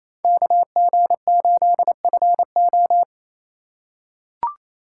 Morse code, heard as a keyed tone, about 28 wpm, 700 Hz. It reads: KG8FO
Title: USA Call Signs - 4x Character Spacing - 28wpm